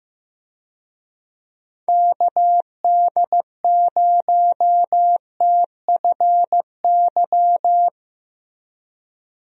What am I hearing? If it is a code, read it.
Morse code, heard as a keyed tone, about 15 wpm, 700 Hz. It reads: KD0TFY
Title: USA Call Signs - 6 Characters - Repeated 2x - 15wpm